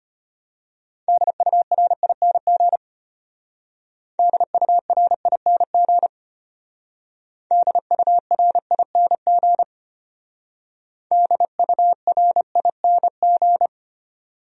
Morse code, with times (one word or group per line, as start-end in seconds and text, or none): 1.1-2.8 DURING
4.2-6.1 DURING
7.5-9.6 DURING
11.1-13.7 DURING